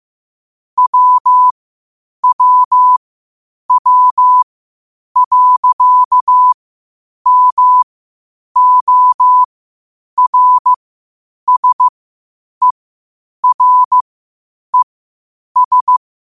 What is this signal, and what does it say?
Morse code, heard as a keyed tone, about 15 wpm, spaced as for 8 wpm, 1000 Hz. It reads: WWW.MORSERES
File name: qotdfiles / 8wpm